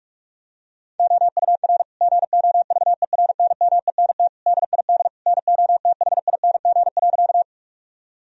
Morse code, heard as a keyed tone, about 41 wpm, 700 Hz. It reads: OUR GOVERNMENT DID NOTHING.
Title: Sentences from Top 1000 Words - Review Part 3 - 45wpm